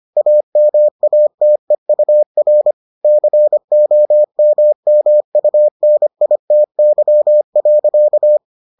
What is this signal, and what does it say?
Morse code, heard as a keyed tone, about 25 wpm, 600 Hz. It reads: AMATEUR COMMUNITY.